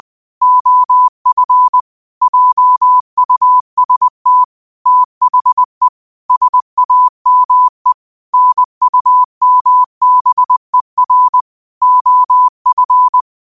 Morse code, heard as a keyed tone, about 20 wpm, 1000 Hz. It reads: OF JUST THE SAME NUMBER OF
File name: qotdfiles / 20wpm